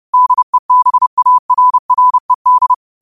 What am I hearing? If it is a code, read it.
Morse code, heard as a keyed tone, about 30 wpm, 1000 Hz. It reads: DEBARRED
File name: qotdfiles / 30wpm